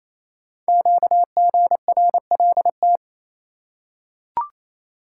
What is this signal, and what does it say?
Morse code, heard as a keyed tone, about 28 wpm, 700 Hz. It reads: QGRLT
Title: Code Groups - 5 Letters - 28wpm